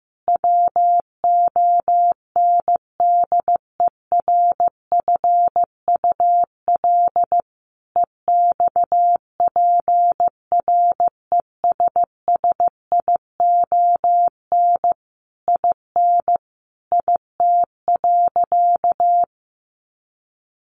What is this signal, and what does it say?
Morse code, heard as a keyed tone, about 15 wpm, 700 Hz. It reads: WONDERFUL EXPRESSION IN IT.
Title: Sentences from Top 1200 Words 15wpm